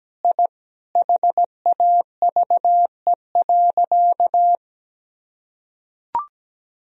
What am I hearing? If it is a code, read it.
Morse code, heard as a keyed tone, about 17 wpm, 700 Hz. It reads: I HAVE.
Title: Sentences from Top 1000 Words - Review Part 1 - 17wpm